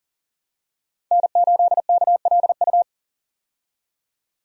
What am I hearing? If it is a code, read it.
Morse code, heard as a keyed tone, about 40 wpm, 700 Hz. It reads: N8KLU
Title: USA Call Signs - 5 Characters - 40wpm